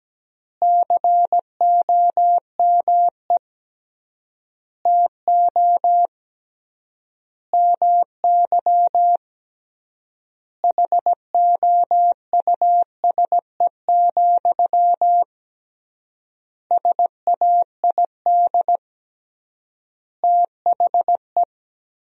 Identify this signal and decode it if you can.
Morse code, heard as a keyed tone, about 17 wpm, 700 Hz. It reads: COME TO MY HOUSE, SAID THE
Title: 3x Word-Spacing - Sentences from Top 400 Words - 17wpm